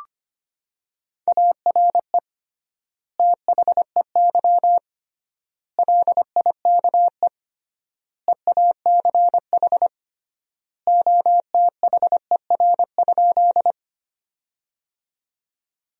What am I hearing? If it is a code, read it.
Morse code, heard as a keyed tone, about 25 wpm, 700 Hz. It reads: ARE THEY LIKE EACH OTHER?
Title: 3x Word-Spacing - Sentences from Top 200 Words - 25wpm